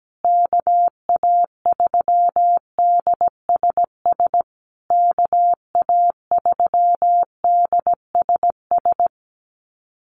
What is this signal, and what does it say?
Morse code, heard as a keyed tone, about 17 wpm, 700 Hz. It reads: KA3DSS KA3DSS
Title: USA Call Signs Repeated 3x 17wpm